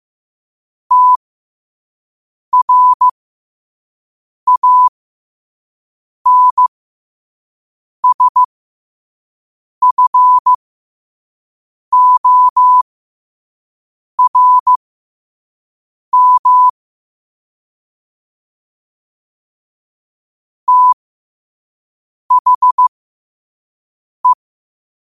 Morse code, heard as a keyed tone, about 15 wpm, spaced as for 5 wpm, 1000 Hz. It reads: TRANSFORM THE
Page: New quotes every day in morse code at 5 Words per minute.